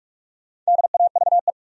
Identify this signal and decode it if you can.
Morse code, heard as a keyed tone, about 46 wpm, 700 Hz. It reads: DAVE
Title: USA Common Names - 45wpm